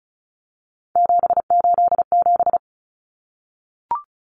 Morse code, heard as a keyed tone, about 34 wpm, 700 Hz. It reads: 787
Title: Common Cut Numbers - 3-Digit 35wpm